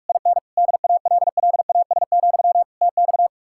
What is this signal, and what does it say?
Morse code, heard as a keyed tone, about 57 wpm, 700 Hz. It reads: IN DALLAS, TX